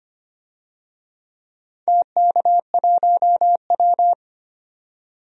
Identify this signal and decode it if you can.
Morse code, heard as a keyed tone, about 25 wpm, 700 Hz. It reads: TK1W
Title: DXCC-100 Call Signs - 4 Characters - 25wpm